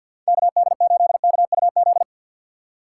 Morse code, heard as a keyed tone, about 47 wpm, 700 Hz. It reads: KD8KUZ
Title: USA Call Signs - 6 Characters - 3x Character Spacing - 50wpm